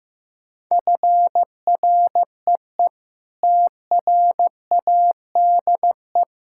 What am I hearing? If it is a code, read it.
Morse code, heard as a keyed tone, about 15 wpm, 700 Hz. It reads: FREE TRADE
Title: Sentences from Top 1000 Words - Review Part 6 - 15wpm